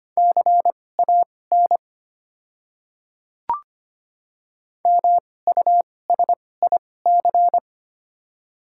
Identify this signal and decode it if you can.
Morse code, heard as a keyed tone, about 25 wpm, spaced as for 18 wpm, 700 Hz. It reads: CAN MUSIC
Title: 2x Character Spacing - Top 300 Words - 25wpm